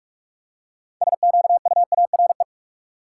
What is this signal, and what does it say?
Morse code, heard as a keyed tone, about 46 wpm, 700 Hz. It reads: SQUARE
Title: Six Letter Words - 45wpm